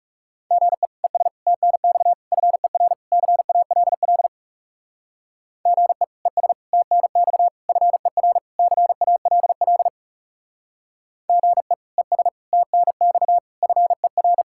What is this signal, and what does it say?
Morse code, heard as a keyed tone, about 37 wpm, 700 Hz. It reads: GE ES TNX FER CALL GE ES TNX FER CALL GE ES TNX FER
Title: Speed-Racing Common QSO Phrases Encore - 30wpm